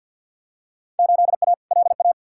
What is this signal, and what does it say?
Morse code, heard as a keyed tone, about 47 wpm, 700 Hz. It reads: 8A LA
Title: Field Day Exchanges Encore - 50wpm